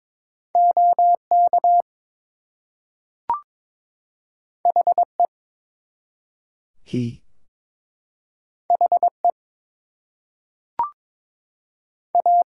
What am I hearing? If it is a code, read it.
Morse code, heard as a keyed tone, about 22 wpm, 700 Hz. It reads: OK HE HE A
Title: Two Letter Words - 22wpm